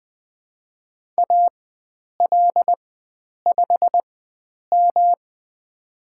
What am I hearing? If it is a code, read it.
Morse code, heard as a keyed tone, about 20 wpm, spaced as for 9 wpm, 700 Hz. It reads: AL5M
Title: USA Call Signs - 4 Characters - 4x Character Spacing - 20wpm